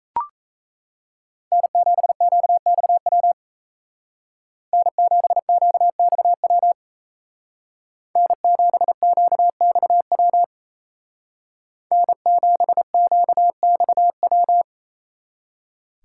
Morse code, heard as a keyed tone, about 33 wpm, 700 Hz. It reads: N7QXW N7QXW N7QXW N7QXW
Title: Speed Racing - USA 1x3 Call Signs 28wpm